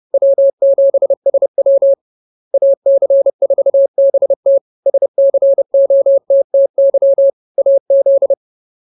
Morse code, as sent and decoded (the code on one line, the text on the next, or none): .-- --... ... .-- / .- -.-. ....- -... - / ... -.-. --- - - -.-- / .- --..
W7SW AC4BT SCOTTY AZ